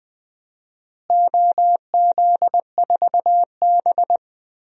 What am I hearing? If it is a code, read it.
Morse code, heard as a keyed tone, about 20 wpm, 700 Hz. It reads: OZ4B
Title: DXCC-100 Call Signs - 4 Characters - Repeated 2x - 20wpm